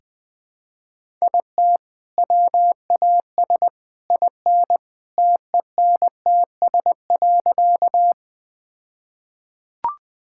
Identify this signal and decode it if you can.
Morse code, heard as a keyed tone, about 21 wpm, 700 Hz. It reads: IT WAS IN TENTS.
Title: All-Age Appropriate Jokes - 20wpm